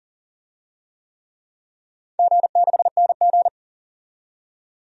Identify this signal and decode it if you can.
Morse code, heard as a keyed tone, about 39 wpm, 700 Hz. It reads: G6NG
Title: DXCC-100 Call Signs - 4 Characters - 40wpm